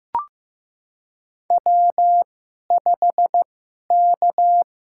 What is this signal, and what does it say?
Morse code, heard as a keyed tone, about 15 wpm, spaced as for 11 wpm, 700 Hz. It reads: W5K
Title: USA Call Signs - 2x Character Spacing - 15wpm